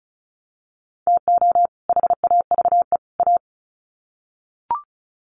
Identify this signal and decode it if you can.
Morse code, heard as a keyed tone, about 34 wpm, 700 Hz. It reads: TO HAVE A
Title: Sets of 3 Words 35wpm